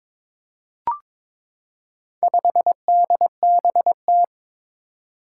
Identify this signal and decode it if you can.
Morse code, heard as a keyed tone, about 21 wpm, 700 Hz. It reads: 5DBT
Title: Four Challenging Letter-Number Combinations 22wpm